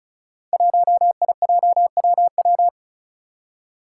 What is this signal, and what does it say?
Morse code, heard as a keyed tone, about 36 wpm, 700 Hz. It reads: 1IJWW
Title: Five Challenging Letter-Number Combinations 35wpm